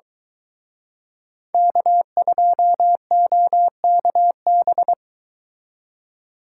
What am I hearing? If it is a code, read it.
Morse code, heard as a keyed tone, about 23 wpm, 700 Hz. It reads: K2OKB